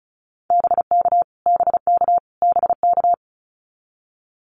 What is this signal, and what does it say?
Morse code, heard as a keyed tone, about 34 wpm, 700 Hz. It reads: BK BK BK
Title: Abbreviations Repeated 3x - 35wpm